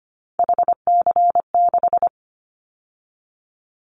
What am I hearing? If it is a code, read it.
Morse code, heard as a keyed tone, about 25 wpm, 700 Hz. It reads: HC6
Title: Callsign Prefixes for Top 100 DXCC - 25wpm